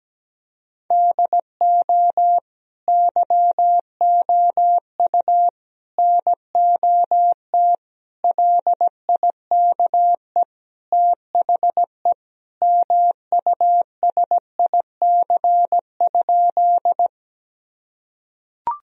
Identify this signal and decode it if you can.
Morse code, heard as a keyed tone, about 17 wpm, 700 Hz. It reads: DO YOU NOT LIKE THE MUSIC?
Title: Sentences from Top 1000 Words - Review Part 1 - 17wpm